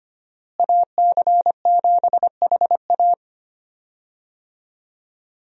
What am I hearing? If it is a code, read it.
Morse code, heard as a keyed tone, about 25 wpm, 700 Hz. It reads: AC7HA
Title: USA Call Signs - 5 Characters - Repeated 2x - 25wpm